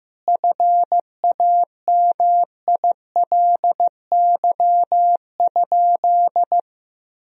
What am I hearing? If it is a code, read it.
Morse code, heard as a keyed tone, about 15 wpm, 700 Hz. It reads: FAMILY?
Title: Sentences from Top 1000 Words - Review Part 1 - 15wpm